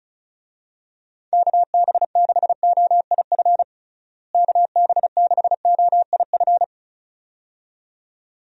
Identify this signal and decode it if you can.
Morse code, heard as a keyed tone, about 35 wpm, 700 Hz. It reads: KB6OIF KB6OIF
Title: USA Call Signs - 6 Characters - Repeated 2x - 35wpm